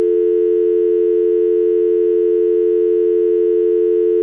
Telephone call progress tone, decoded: dial tone